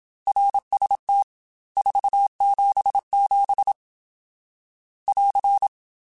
Morse code, heard as a keyed tone, about 27 wpm, 800 Hz. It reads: RST 477 <AR>
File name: RSTs / RST-mp3s / Your RSTs @25